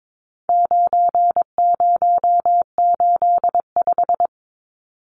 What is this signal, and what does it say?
Morse code, heard as a keyed tone, about 22 wpm, 700 Hz. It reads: 9085
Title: Numbers - 4-Digit 22wpm